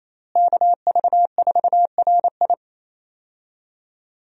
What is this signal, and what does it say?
Morse code, heard as a keyed tone, about 28 wpm, 700 Hz. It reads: KV4RI
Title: USA Call Signs - 5 Characters - Repeated 3x - 28wpm